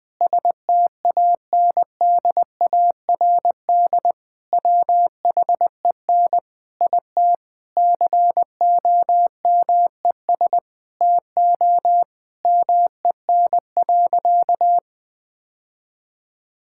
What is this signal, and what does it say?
Morse code, heard as a keyed tone, about 20 wpm, 700 Hz. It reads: STANDARD WHEN IT COMES TO MEN.
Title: Sentences from Top 1000 Words - Review Part 6 - 20wpm